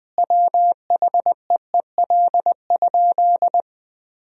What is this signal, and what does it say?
Morse code, heard as a keyed tone, about 21 wpm, 700 Hz. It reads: WHEEL?
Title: Sentences from Top 1000 Words - Review Part 3 - 20wpm